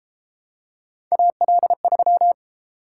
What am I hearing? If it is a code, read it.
Morse code, heard as a keyed tone, about 30 wpm, 700 Hz. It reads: AL3